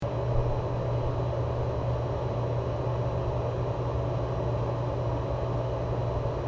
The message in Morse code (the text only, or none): none